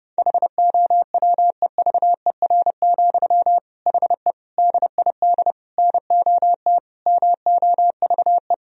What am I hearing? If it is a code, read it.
Morse code, heard as a keyed tone, about 30 wpm, 700 Hz. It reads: HOWEVER, HE DID NOT MOVE
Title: Sentences from Top 1000 Words - Review Part 3 - 30wpm